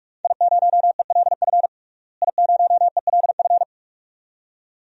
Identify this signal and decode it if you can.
Morse code, heard as a keyed tone, about 46 wpm, 700 Hz. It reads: I0ELF I0ELF
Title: DXCC-100 Call Signs - 5 Characters - Repeated 2x - 45wpm